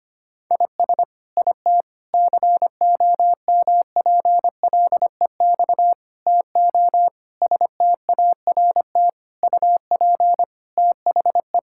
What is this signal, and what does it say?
Morse code, heard as a keyed tone, about 25 wpm, 700 Hz. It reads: IS IT COMPLEX TO START UP THE
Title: Sentences from Top 1000 Words - Review Part 6 - 25wpm